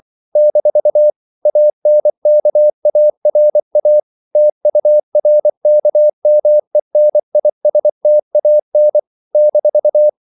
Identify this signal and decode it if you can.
Morse code, heard as a keyed tone, about 24 wpm, 600 Hz. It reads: - ANKARA TURKMENISTAN -
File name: CW / Practice / NationalCapitals24